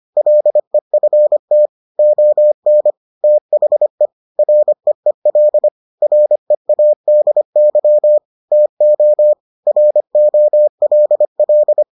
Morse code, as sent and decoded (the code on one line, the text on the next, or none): .-.. ..-. - / --- -. / - .... / .-. . .-.. / .-. .- -.. -.-- / - --- / .-. --- .-.. .-..
LEFT ON THE REEL READY TO ROLL